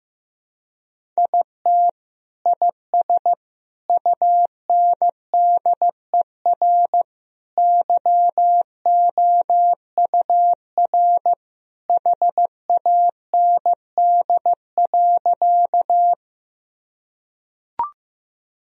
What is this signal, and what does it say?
Morse code, heard as a keyed tone, about 15 wpm, 700 Hz. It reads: IT IS UNDER YOUR HAND.
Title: Sentences from Top 1000 Words - Review Part 1 - 15wpm